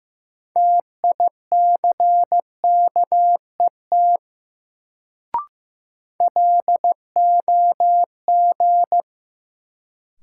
Morse code, heard as a keyed tone, about 16 wpm, 700 Hz. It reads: TICKET LOG
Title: Sentences from Top 1400 Words - New Word Drill Alt - 15wpm